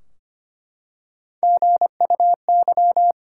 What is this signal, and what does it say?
Morse code, heard as a keyed tone, about 25 wpm, 700 Hz. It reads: GUY
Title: Three Letter Words - 25wpm